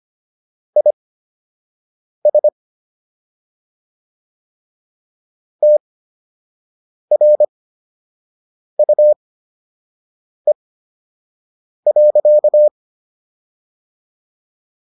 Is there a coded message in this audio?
Yes: Morse code from 0.8 to 12.7 s, heard as a keyed tone, about 25 wpm, spaced as for 6 wpm, 600 Hz. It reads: IS TRUE.